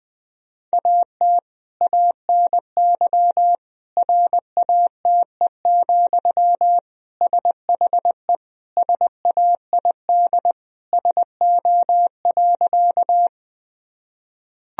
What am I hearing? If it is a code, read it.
Morse code, heard as a keyed tone, about 20 wpm, 700 Hz. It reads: AT ANY RATE, SHE SAID SO.
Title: Sentences from Top 1000 Words - Review Part 4 - 20wpm